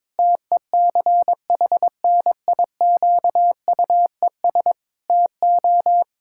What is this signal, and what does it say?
Morse code, heard as a keyed tone, about 22 wpm, 700 Hz. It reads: TECHNIQUES TO
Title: Pangram Challenge v2 - 22wpm